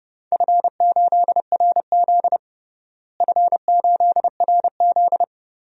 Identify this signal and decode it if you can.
Morse code, heard as a keyed tone, about 30 wpm, 700 Hz. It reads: F8RZ F8RZ
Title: DXCC-100 Call Signs - 4 Characters - Repeated 3x - 30wpm